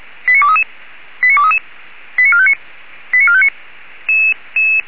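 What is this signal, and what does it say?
Five-tone selective call (ZVEI), heard as quick runs of five tones, tones of about 80 ms. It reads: 89233 89233 89459 89459
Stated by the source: ZVEI five-tone selective call 89241 received on an SDR.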